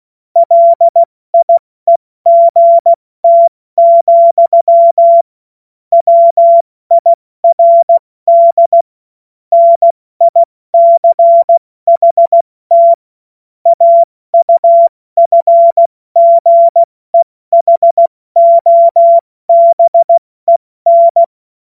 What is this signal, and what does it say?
Morse code, heard as a keyed tone, about 16 wpm, 680 Hz. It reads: LIEGT, WIRD NICHT AUFGEHOBEN
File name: Morsekurs / Witze / Witz 17. August 2021, 70 F80